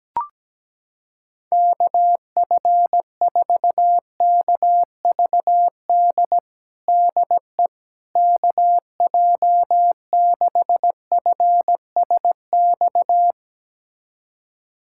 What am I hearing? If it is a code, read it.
Morse code, heard as a keyed tone, about 17 wpm, 700 Hz. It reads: KF4KVD DE KJ6FSX
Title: USA Callsign DE Callsign - 6 Characters - 17wpm